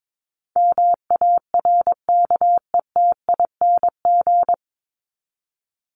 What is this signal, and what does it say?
Morse code, heard as a keyed tone, about 22 wpm, 700 Hz. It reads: MARKETING
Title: Top 751-1000 Words 22wpm